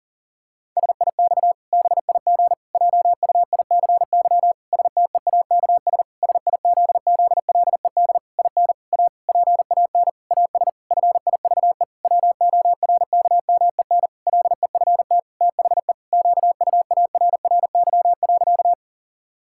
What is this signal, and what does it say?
Morse code, heard as a keyed tone, about 37 wpm, 700 Hz. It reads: SIX BIG JUICY STEAKS SIZZLED IN A PAN AS FIVE WORKMEN LEFT THE QUARRY.
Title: Pangram Challenge v2 - 40wpm